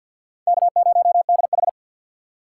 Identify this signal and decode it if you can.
Morse code, heard as a keyed tone, about 48 wpm, 700 Hz. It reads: K0DH
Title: USA Call Signs - 4 Characters - 6x Character Spacing - 50wpm